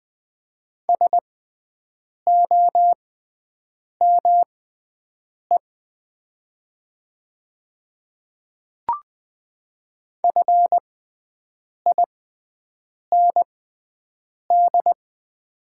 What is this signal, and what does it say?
Morse code, heard as a keyed tone, about 20 wpm, spaced as for 7 wpm, 700 Hz. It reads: SOME FIND
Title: 6x Character Spacing - Top 100 Words - 20wpm